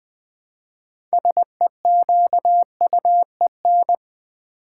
Morse code, heard as a keyed tone, about 20 wpm, 700 Hz. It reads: SEQUEN
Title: Build A Word - 2 Letters At A Time - 8-Letter Words - 20wpm